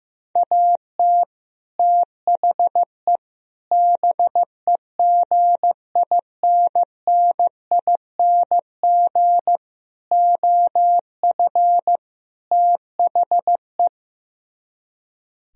Morse code, heard as a keyed tone, about 15 wpm, 700 Hz. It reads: AT THE BEGINNING OF THE
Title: Speed-Racing - Sets of 5 Words 15wpm